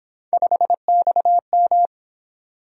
Morse code, heard as a keyed tone, about 26 wpm, 700 Hz. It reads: 5XM
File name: Speed-Racing - Three Letter-Number Combinations - 22wpm